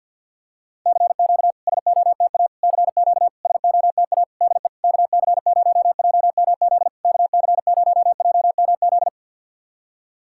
Morse code, heard as a keyed tone, about 50 wpm, 700 Hz. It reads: CQ SOTA CQ SOTA DE KC0JMZ KC0JMZ
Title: SOTA QSO Practice 50wpm